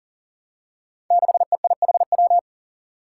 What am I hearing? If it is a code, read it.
Morse code, heard as a keyed tone, about 40 wpm, 700 Hz. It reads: 6EIHW
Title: Five Challenging Letter-Number Combinations 40wpm